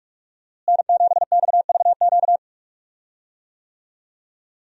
Morse code, heard as a keyed tone, about 45 wpm, 700 Hz. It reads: N7XVQ
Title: USA Call Signs - 5 Characters - Repeated 2x - 45wpm